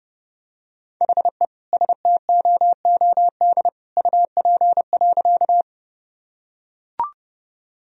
Morse code, heard as a keyed tone, about 30 wpm, 700 Hz. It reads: HE STOOD UP.
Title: Sentences from Top 1000 Words - Review Part 2 - 30wpm